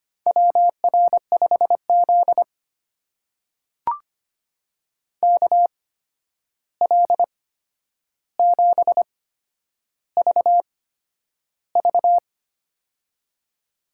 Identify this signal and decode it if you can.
Morse code, heard as a keyed tone, about 25 wpm, spaced as for 10 wpm, 700 Hz. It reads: WR5Z KL7VV
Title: USA Call Signs - 8x Character Spacing - 25wpm